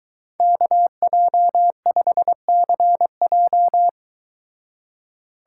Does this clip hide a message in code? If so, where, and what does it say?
Yes: Morse code from 0.4 to 3.9 s, heard as a keyed tone, about 23 wpm, 700 Hz. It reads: KJ5CJ